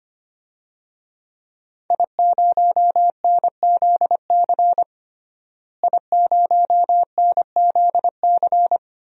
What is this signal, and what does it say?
Morse code, heard as a keyed tone, about 25 wpm, 700 Hz. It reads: I0NZC I0NZC
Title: DXCC-100 Call Signs - 5 Characters - Repeated 3x - 25wpm